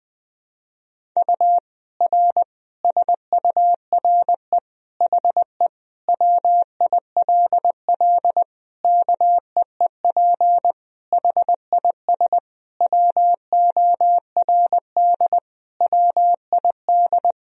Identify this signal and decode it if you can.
Morse code, heard as a keyed tone, about 20 wpm, 700 Hz. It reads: U R SURE HE WILL KEEP HIS WORD WID
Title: Sentences from Top 300 Words and Abbreviations 20wpm